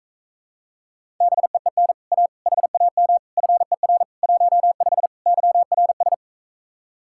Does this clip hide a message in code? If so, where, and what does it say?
Yes: Morse code from 1.2 to 6.1 s, heard as a keyed tone, about 45 wpm, 700 Hz. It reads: BEEN A HAM FER 15 YRS